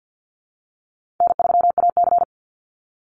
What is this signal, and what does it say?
Morse code, heard as a keyed tone, about 48 wpm, 700 Hz. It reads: N3AC
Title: Easy Call Signs Repeated 2x - 50wpm